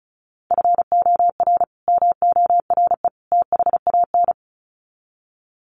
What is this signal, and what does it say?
Morse code, heard as a keyed tone, about 34 wpm, 700 Hz. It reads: FOR MORE THAN
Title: Sets of 3 Words 35wpm